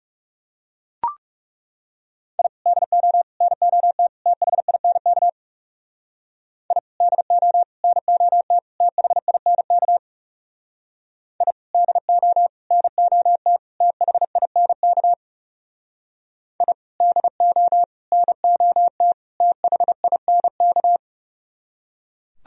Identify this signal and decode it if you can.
Morse code, heard as a keyed tone, about 37 wpm, 700 Hz. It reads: I DO NOT THINK I DO NOT THINK I DO NOT THINK I DO NOT THINK
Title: Speed-Racing - Sets of 4 Words 30wpm